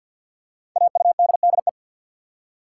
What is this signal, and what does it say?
Morse code, heard as a keyed tone, about 51 wpm, 700 Hz. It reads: AUDDE
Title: Code Groups - 5 Letters - 50wpm